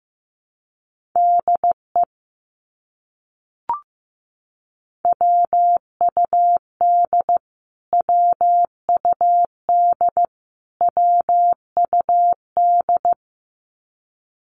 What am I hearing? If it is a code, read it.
Morse code, heard as a keyed tone, about 15 wpm, 700 Hz. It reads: DE WUD WUD WUD
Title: Abbreviations Repeated 3x - 15wpm